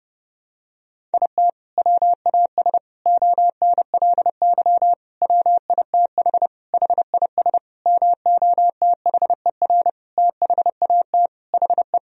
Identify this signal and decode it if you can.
Morse code, heard as a keyed tone, about 32 wpm, 700 Hz. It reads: IT WAS ONLY WITH HIS MOTHER THAT HE
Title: Sentences from Top 1000 Words - Review Part 1 - 30wpm